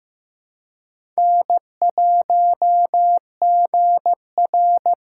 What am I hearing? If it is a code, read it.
Morse code, heard as a keyed tone, about 15 wpm, 700 Hz. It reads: N1GR
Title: USA Callsign DE Callsign - 4 Characters - 15wpm